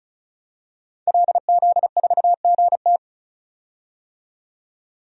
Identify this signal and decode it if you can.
Morse code, heard as a keyed tone, about 34 wpm, 700 Hz. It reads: LZ4GT
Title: DXCC-100 Call Signs - 5 Characters - 35wpm